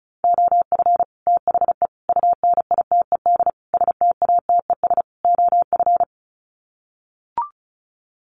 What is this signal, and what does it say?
Morse code, heard as a keyed tone, about 34 wpm, 700 Hz. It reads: OF THE UNITED STATES OF
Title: Sets of 5 Words Encore 35wpm